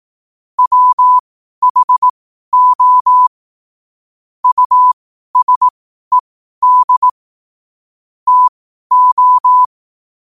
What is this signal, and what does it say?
Morse code, heard as a keyed tone, about 18 wpm, spaced as for 12 wpm, 1000 Hz. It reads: WHO USED TO